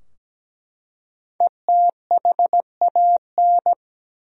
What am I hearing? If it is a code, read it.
Morse code, heard as a keyed tone, about 17 wpm, 700 Hz. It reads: ETHAN
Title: USA Common Names - 17wpm